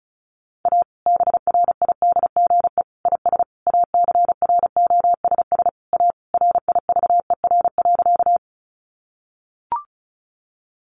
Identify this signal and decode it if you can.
Morse code, heard as a keyed tone, about 34 wpm, 700 Hz. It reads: A BRIDGE IS ACROSS A RIVER.
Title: Sentences from Top 1200 Words 35wpm